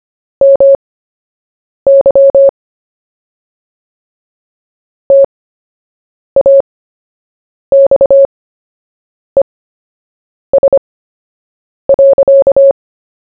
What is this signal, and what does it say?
Morse code, heard as a keyed tone, about 25 wpm, spaced as for 7 wpm, 550 Hz. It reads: MY TAXES.
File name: cw / fists / Short-Story-1-7wpm